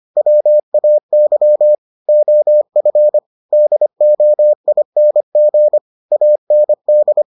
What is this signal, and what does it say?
Morse code, heard as a keyed tone, about 25 wpm, 600 Hz. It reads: WAY OF DOING AND